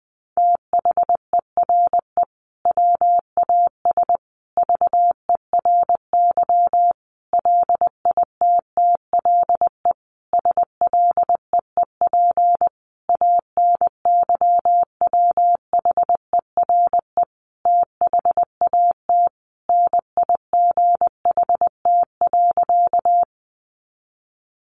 Morse code, heard as a keyed tone, about 20 wpm, 700 Hz. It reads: THERE WAS VERY LITTLE SLEEP ANYWHERE THAT NIGHT.
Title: Sentences from Top 1400 Words 20wpm